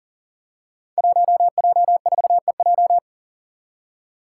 Morse code, heard as a keyed tone, about 40 wpm, 700 Hz. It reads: 1J4EJ
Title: Five Challenging Letter-Number Combinations 40wpm